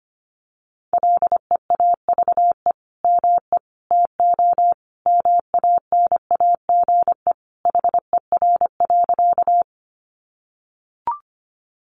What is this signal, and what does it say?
Morse code, heard as a keyed tone, about 25 wpm, 700 Hz. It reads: LEAVE ME TO MANAGE HER.
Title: Sentences from Top 1400 Words 25wpm